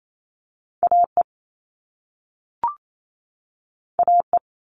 Morse code, heard as a keyed tone, about 28 wpm, 700 Hz. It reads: AE AE
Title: All Cut Numbers - 2-Digit 28wpm